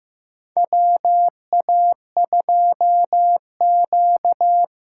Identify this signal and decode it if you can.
Morse code, heard as a keyed tone, about 15 wpm, 700 Hz. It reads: WA2Q
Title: USA Call Signs - 6 Characters - 8x Character Spacing - 15wpm